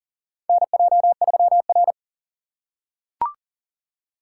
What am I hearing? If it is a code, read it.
Morse code, heard as a keyed tone, about 40 wpm, 700 Hz. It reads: NJ3R
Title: USA Call Signs - 6x Character Spacing - 40wpm